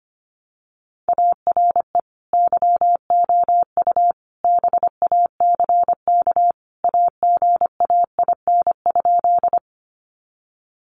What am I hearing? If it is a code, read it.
Morse code, heard as a keyed tone, about 25 wpm, 700 Hz. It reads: ARE YOU BACK AGAIN?
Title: Sentences from Top 200 Words 25wpm